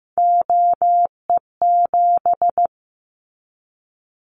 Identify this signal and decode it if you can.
Morse code, heard as a keyed tone, about 15 wpm, 700 Hz. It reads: OE7
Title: Callsign Prefixes for Top 100 DXCC - 15wpm